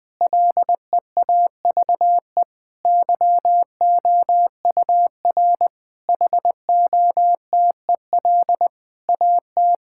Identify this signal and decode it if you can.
Morse code, heard as a keyed tone, about 20 wpm, 700 Hz. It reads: LEAVE YOUR HOTEL AT
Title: Sentences from Top 1600 Words 20wpm